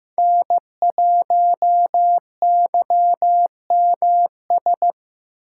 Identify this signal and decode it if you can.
Morse code, heard as a keyed tone, about 15 wpm, 700 Hz. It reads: N1YMS
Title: USA Call Signs - 5 Characters - 6x Character Spacing - 15wpm